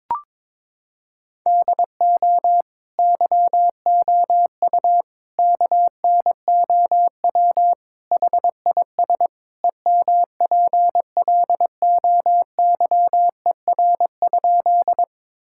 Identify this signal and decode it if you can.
Morse code, heard as a keyed tone, about 22 wpm, 700 Hz. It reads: DO YOU KNOW HIS EMPLOYER?
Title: Sentences from Top 1600 Words 22wpm